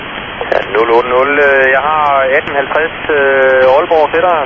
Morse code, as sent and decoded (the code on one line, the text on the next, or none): none